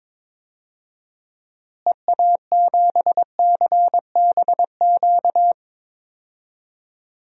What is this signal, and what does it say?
Morse code, heard as a keyed tone, about 22 wpm, 700 Hz. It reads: EA7CBQ
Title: DXCC-100 Call Signs - 6 Characters - Repeated 3x - 22wpm